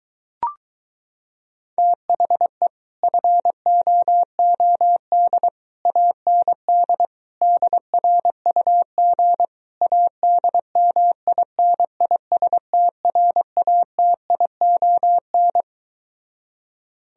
Morse code, heard as a keyed tone, about 22 wpm, 700 Hz. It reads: THE FOOD AND DRUG ADMINISTRATION